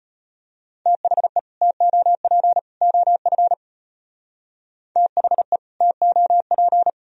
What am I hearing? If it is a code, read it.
Morse code, heard as a keyed tone, about 31 wpm, 700 Hz. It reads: THE TOP OF THE TOP
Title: Speed-Racing - Sets of 3 Words 25wpm